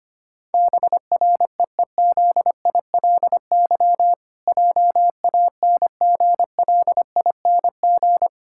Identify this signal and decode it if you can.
Morse code, heard as a keyed tone, about 25 wpm, 700 Hz. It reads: BREEZILY JANGLING
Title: Pangram Challenge v2 - 25wpm